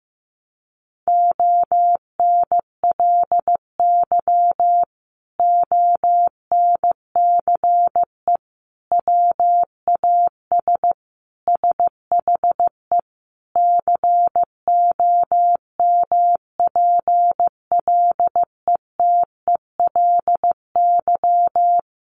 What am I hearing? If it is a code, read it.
Morse code, heard as a keyed tone, about 15 wpm, 700 Hz. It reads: ONLY ONCE WAS SHE COMPLETELY
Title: Sentences from Top 1100 Words 15wpm